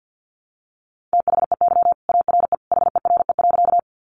Morse code, heard as a keyed tone, about 50 wpm, 700 Hz. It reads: THEY ARE HERE.
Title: Sentences from Top 200 Words 50wpm